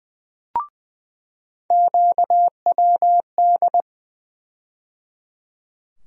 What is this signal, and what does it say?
Morse code, heard as a keyed tone, about 20 wpm, 700 Hz. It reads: QWD
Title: USA Call Sign Postfixes - 3 Letters 20wpm